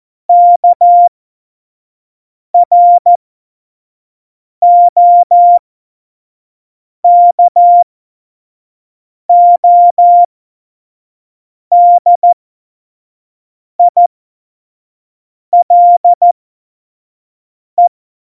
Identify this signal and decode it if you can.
Morse code, heard as a keyed tone, about 14 wpm, spaced as for 5 wpm, 700 Hz. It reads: KROKODILE
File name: Morsekurs / Witze / Witz 16. April 2019, 25 F70